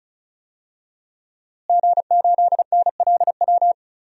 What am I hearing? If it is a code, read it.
Morse code, heard as a keyed tone, about 36 wpm, 700 Hz. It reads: G8NLW
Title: DXCC-100 Call Signs - 5 Characters - Repeated 2x - 35wpm